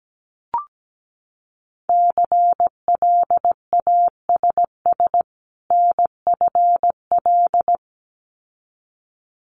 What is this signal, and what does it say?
Morse code, heard as a keyed tone, about 17 wpm, 700 Hz. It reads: CLASS NFL
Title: Winter Field Day Exchanges 17wpm